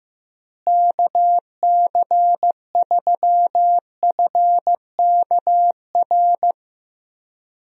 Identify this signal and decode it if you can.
Morse code, heard as a keyed tone, about 15 wpm, 700 Hz. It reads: KC3FKR
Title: USA Call Signs - 6 Characters - Repeated 2x - 15wpm